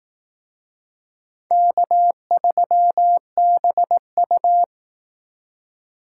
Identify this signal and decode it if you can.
Morse code, heard as a keyed tone, about 18 wpm, 700 Hz. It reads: K3BU
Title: Speed Racing - USA Call Signs 15wpm